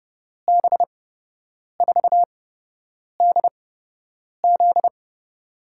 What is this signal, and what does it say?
Morse code, heard as a keyed tone, about 30 wpm, spaced as for 8 wpm, 700 Hz. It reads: B4DZ
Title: USA Call Signs - 5 Characters - 8x Character Spacing - 30wpm